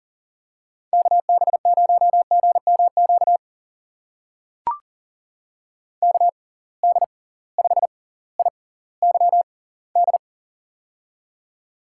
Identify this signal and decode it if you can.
Morse code, heard as a keyed tone, about 40 wpm, spaced as for 20 wpm, 700 Hz. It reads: KB0GMQ KD5IYD
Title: USA Call Signs - 6 Characters - 6x Character Spacing - 40wpm